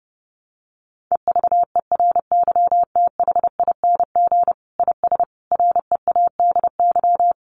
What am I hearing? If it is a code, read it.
Morse code, heard as a keyed tone, about 30 wpm, 700 Hz. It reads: EVERYTHING IS READY
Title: Sentences from Top 700 Words 30wpm